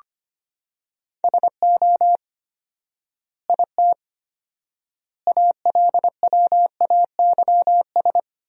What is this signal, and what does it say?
Morse code, heard as a keyed tone, about 25 wpm, 700 Hz. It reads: SO IT ALWAYS
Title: 4x Word-Spacing - Sentences from Top 300 Words - 25wpm